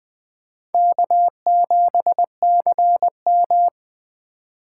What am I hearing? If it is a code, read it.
Morse code, heard as a keyed tone, about 20 wpm, 700 Hz. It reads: K7CM
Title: USA Call Signs - 4 Characters - 4x Character Spacing - 20wpm